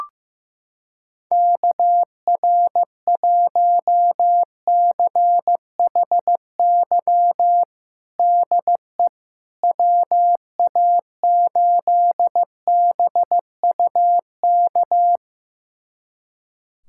Morse code, heard as a keyed tone, about 16 wpm, 700 Hz. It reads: KR1CHY DE WA8BUK
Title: USA Callsign DE Callsign - 6 Characters - 15wpm